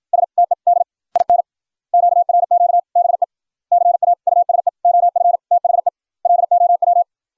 Morse code, heard as a keyed tone, about 56 wpm, spaced as for 45 wpm, 700 Hz. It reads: STED IN QRQ BECAUSE OF THE LOW